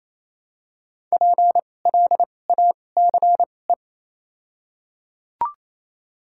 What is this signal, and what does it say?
Morse code, heard as a keyed tone, about 28 wpm, spaced as for 20 wpm, 700 Hz. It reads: PLACE
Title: 2x Character Spacing - Top 500 Words - 28wpm